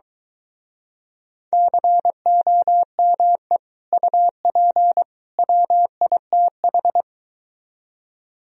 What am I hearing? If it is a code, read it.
Morse code, heard as a keyed tone, about 24 wpm, 700 Hz. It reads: COME UP WITH